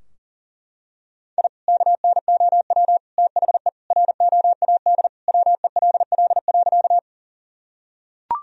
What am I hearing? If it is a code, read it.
Morse code, heard as a keyed tone, about 38 wpm, 700 Hz. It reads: I KNOW THE ROAD WELL.
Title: Sentences from Top 1000 Words - Review Part 2 - 40wpm